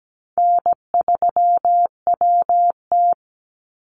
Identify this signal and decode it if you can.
Morse code, heard as a keyed tone, about 17 wpm, 700 Hz. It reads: N3WT
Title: Call Signs Repeated 3x - 17wpm